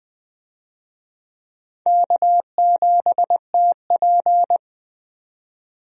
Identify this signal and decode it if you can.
Morse code, heard as a keyed tone, about 20 wpm, 700 Hz. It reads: K7TP
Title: USA Call Signs - 4 Characters - 20wpm